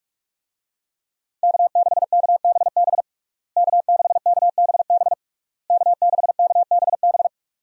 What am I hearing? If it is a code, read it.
Morse code, heard as a keyed tone, about 41 wpm, 700 Hz. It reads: K6KBB K6KBB K6KBB
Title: USA Call Signs - 5 Characters - Repeated 3x - 45wpm